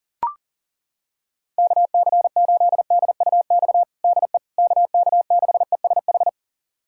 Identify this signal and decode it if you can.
Morse code, heard as a keyed tone, about 40 wpm, 700 Hz. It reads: KC8DUX DE KK6ESH
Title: USA Callsign DE Callsign 40wpm